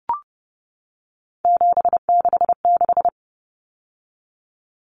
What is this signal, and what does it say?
Morse code, heard as a keyed tone, about 30 wpm, 700 Hz. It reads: 766
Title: Numbers - 3-Digit 30wpm